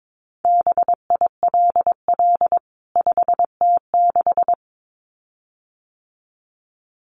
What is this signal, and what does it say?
Morse code, heard as a keyed tone, about 22 wpm, 700 Hz. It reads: BILL 5T6
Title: CWT Exchange 22wpm